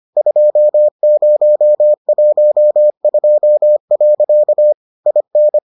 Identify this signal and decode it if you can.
Morse code, heard as a keyed tone, about 25 wpm, 600 Hz. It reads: 2012. IN